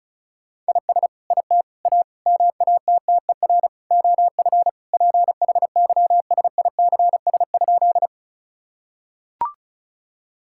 Morse code, heard as a keyed tone, about 34 wpm, 700 Hz. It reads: IS IT A MATTER OF PHYSICS?
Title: Sentences from Top 1000 Words - Review Part 5 - 35wpm